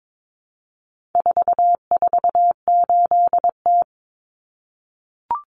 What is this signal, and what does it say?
Morse code, heard as a keyed tone, about 22 wpm, 700 Hz. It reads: 448T
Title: Common Cut Numbers - 4-Digit 22wpm